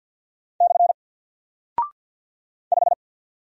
Morse code, heard as a keyed tone, about 48 wpm, 700 Hz. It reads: / 5
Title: Single Character (Letter:Number:Punctuation:Prosign) Alt 50wpm